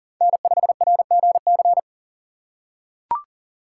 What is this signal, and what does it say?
Morse code, heard as a keyed tone, about 40 wpm, 700 Hz. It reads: N5RGC
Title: USA Call Signs - 5 Characters - 6x Character Spacing - 40wpm